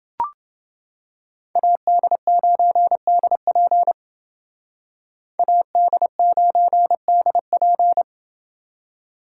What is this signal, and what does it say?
Morse code, heard as a keyed tone, about 29 wpm, 700 Hz. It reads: AD9DP AD9DP